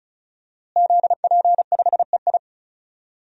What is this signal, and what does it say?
Morse code, heard as a keyed tone, about 34 wpm, 700 Hz. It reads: ZP5EI
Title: DXCC-100 Call Signs - 5 Characters - Repeated 2x - 35wpm